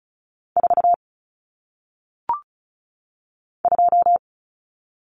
Morse code, heard as a keyed tone, about 36 wpm, 700 Hz. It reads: 4 2
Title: Numbers - 1-Digit 35wpm